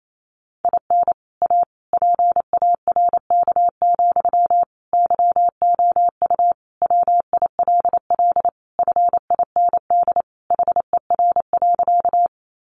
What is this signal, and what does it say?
Morse code, heard as a keyed tone, about 28 wpm, 700 Hz. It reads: IN A PARK, YOU WILL FIND HER.
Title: Sentences from Top 1300 Words 28wpm